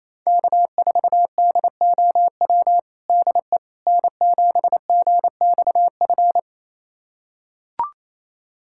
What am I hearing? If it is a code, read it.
Morse code, heard as a keyed tone, about 28 wpm, 700 Hz. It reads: K4DOW DE N7GXF
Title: USA Callsign DE Callsign - 5 Characters - 28wpm